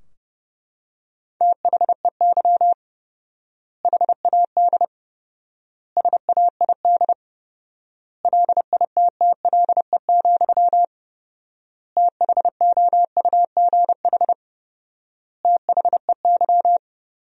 Morse code, heard as a keyed tone, about 32 wpm, 700 Hz. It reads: THEY HAD SAID LITTLE, THOUGH THEY
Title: 4x Word-Spacing - Sentences from Top 400 Words - 30wpm